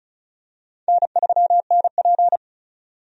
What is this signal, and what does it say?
Morse code, heard as a keyed tone, about 36 wpm, 700 Hz. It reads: N3NP
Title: USA Call Signs - 4 Characters - 35wpm